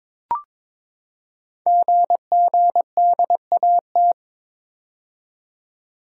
Morse code, heard as a keyed tone, about 22 wpm, 700 Hz. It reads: GGDAT
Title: Code Groups - 5 Letters - 22wpm